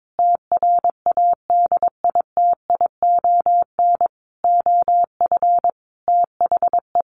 Morse code, heard as a keyed tone, about 22 wpm, 700 Hz. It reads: TRADITION OF THE
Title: Sentences from Top 1500 Words 22wpm